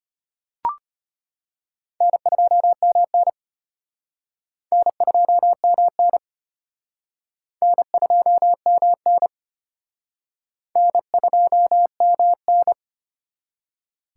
Morse code, heard as a keyed tone, about 33 wpm, 700 Hz. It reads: N2MN N2MN N2MN N2MN